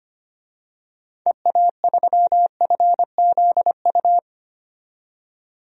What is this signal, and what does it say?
Morse code, heard as a keyed tone, about 25 wpm, 700 Hz. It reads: EA3FZU
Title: DXCC-100 Call Signs - 6 Characters - 25wpm